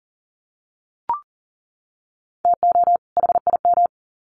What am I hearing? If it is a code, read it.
Morse code, heard as a keyed tone, about 40 wpm, 700 Hz. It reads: TO HIM
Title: Sets of 2 Words 40wpm